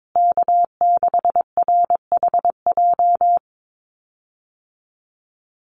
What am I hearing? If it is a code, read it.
Morse code, heard as a keyed tone, about 22 wpm, 700 Hz. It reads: K6RHJ
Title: USA Call Signs Repeated 3x 22wpm